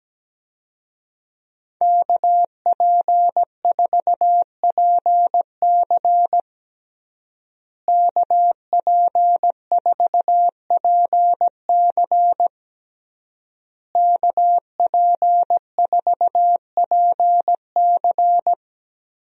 Morse code, heard as a keyed tone, about 17 wpm, 700 Hz. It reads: KP4PC KP4PC KP4PC
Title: DXCC-100 Call Signs - 5 Characters - Repeated 3x - 17wpm